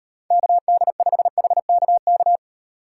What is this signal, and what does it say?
Morse code, heard as a keyed tone, about 38 wpm, 700 Hz. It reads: KD5HKK